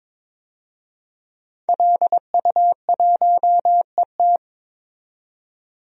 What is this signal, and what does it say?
Morse code, heard as a keyed tone, about 23 wpm, 700 Hz. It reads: LU1ET
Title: DXCC-100 Call Signs - 5 Characters - 22wpm